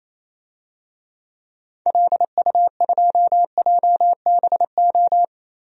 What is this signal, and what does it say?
Morse code, heard as a keyed tone, about 28 wpm, 700 Hz. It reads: LU2JBO
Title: DXCC-100 Call Signs - 6 Characters - 28wpm